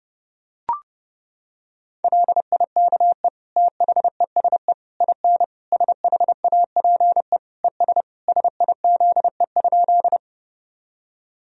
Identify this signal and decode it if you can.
Morse code, heard as a keyed tone, about 30 wpm, 700 Hz. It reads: LIKE THESE IN SHAPE ES SIZE?
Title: Sentences from Top 500 Words and Abbreviations 30wpm